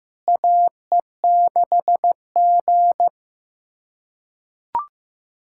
Morse code, heard as a keyed tone, about 15 wpm, 700 Hz. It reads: AE6G
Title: USA Callsign DE Callsign - 4 Characters - 15wpm